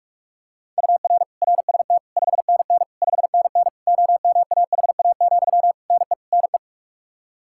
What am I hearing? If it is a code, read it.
Morse code, heard as a keyed tone, about 46 wpm, 700 Hz. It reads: UR RST 5NN 5NN OMAHA, NE NE
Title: Practice QSO - Exchange 1 - Beginning - 45wpm